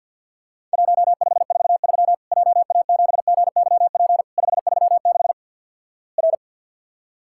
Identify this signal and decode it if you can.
Morse code, heard as a keyed tone, about 50 wpm, 700 Hz. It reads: 1543 JA7GYP 536 R
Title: Worked All Europe DX Contest - QTC Exchange - 50wpm